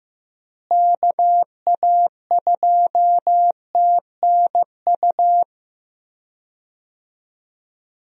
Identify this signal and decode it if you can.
Morse code, heard as a keyed tone, about 15 wpm, 700 Hz. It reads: KA2TNU
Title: USA Call Signs - 6 Characters - Repeated 2x - 15wpm